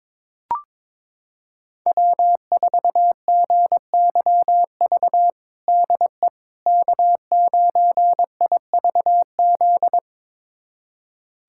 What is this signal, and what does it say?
Morse code, heard as a keyed tone, about 21 wpm, 700 Hz. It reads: W4GYV DE K9IVZ
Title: USA Callsign DE Callsign - 5 Characters - 22wpm